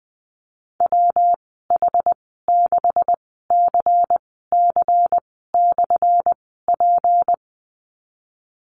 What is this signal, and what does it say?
Morse code, heard as a keyed tone, about 20 wpm, spaced as for 15 wpm, 700 Hz. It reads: WH6CC/P
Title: 179 Lesson 29g - Slash - Call Signs 20wpm